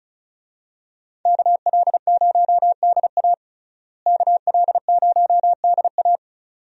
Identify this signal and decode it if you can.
Morse code, heard as a keyed tone, about 34 wpm, 700 Hz. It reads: KL0DA KL0DA
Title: USA Call Signs - 5 Characters - Repeated 2x - 35wpm